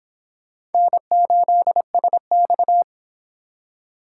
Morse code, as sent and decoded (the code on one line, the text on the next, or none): -. ---.. ... -..-
N8SX